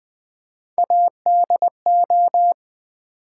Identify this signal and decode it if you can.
Morse code, heard as a keyed tone, about 21 wpm, 700 Hz. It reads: ADO
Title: Build A Word - 3 Letters At A Time - 8-Letter Words - 20wpm